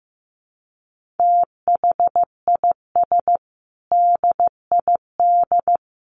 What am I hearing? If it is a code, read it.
Morse code, heard as a keyed tone, about 15 wpm, 700 Hz. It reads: THIS DID
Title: Sentences from Top 1400 Words 15wpm